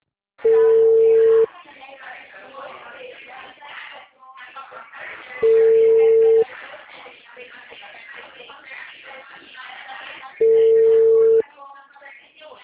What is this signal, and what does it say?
Ringback tone (China, 450 Hz), with speech.